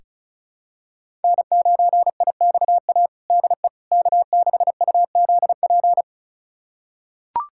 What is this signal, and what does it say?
Morse code, heard as a keyed tone, about 36 wpm, 700 Hz. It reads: N9IXA DE K6UZP
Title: USA Callsign DE Callsign - 5 Characters - 35wpm